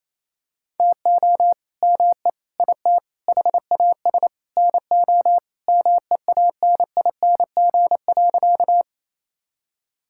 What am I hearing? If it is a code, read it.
Morse code, heard as a keyed tone, about 28 wpm, 700 Hz. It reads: TO ME IT HAS NO MEANING.
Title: Sentences from Top 1000 Words - Review Part 6 - 28wpm